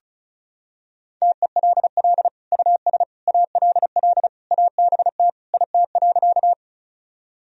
Morse code, heard as a keyed tone, about 36 wpm, 700 Hz. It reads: TELL US ALL ABT IT.
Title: Sentences from Top 200 Words and Abbreviations 35wpm